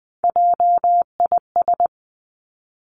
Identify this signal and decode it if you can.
Morse code, heard as a keyed tone, about 20 wpm, 700 Hz. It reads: JIS
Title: Three Challenging Letter-Number Combinations 20wpm